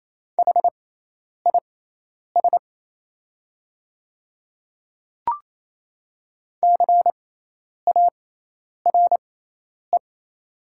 Morse code, heard as a keyed tone, about 28 wpm, spaced as for 10 wpm, 690 Hz. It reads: HIS CARE
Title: 6x Character Spacing - Top 300 Words - 28wpm